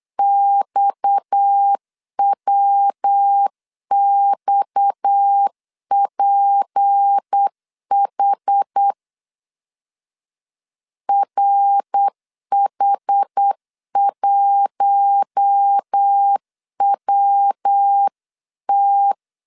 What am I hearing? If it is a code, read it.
Morse code, heard as a keyed tone, about 8 wpm, 800 Hz. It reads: XWXPH RH1WT
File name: LEZIONI / 6 PKX0 / 6 PKX 8wpm 3041